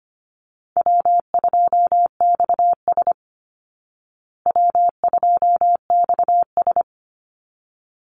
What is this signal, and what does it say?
Morse code, heard as a keyed tone, about 25 wpm, 700 Hz. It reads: W2XS W2XS
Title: Call Signs Repeated 3x - 25wpm